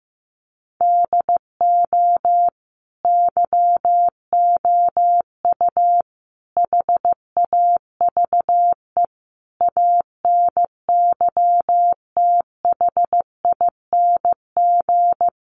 Morse code, heard as a keyed tone, about 15 wpm, 700 Hz. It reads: DO YOU HAVE ANYTHING
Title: Sentences from Top 1000 Words 15wpm